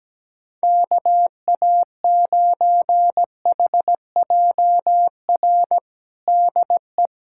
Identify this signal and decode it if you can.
Morse code, heard as a keyed tone, about 17 wpm, 700 Hz. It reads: KA9HJR DE
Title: Practice QSO - Exchange 4 - Ending - 17wpm